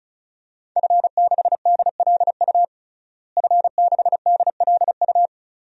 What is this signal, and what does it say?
Morse code, heard as a keyed tone, about 36 wpm, 700 Hz. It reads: F6DLU F6DLU
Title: DXCC-100 Call Signs - 5 Characters - Repeated 2x - 35wpm